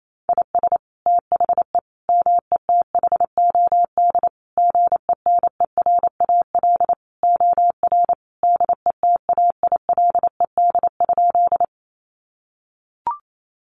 Morse code, heard as a keyed tone, about 28 wpm, 700 Hz. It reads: IS THE METHOD GENERAL OR DETAILED?
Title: Sentences from Top 1500 Words 28wpm